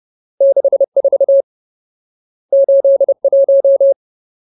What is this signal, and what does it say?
Morse code, heard as a keyed tone, about 30 wpm, 550 Hz. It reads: 64 81
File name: Ultimate Word List / 30WPM Ultimate